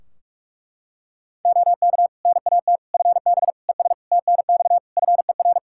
Morse code, heard as a keyed tone, about 41 wpm, 700 Hz. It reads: OK NAT FB ES TNX FER
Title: Practice QSO - Exchange 2 - Info - 45wpm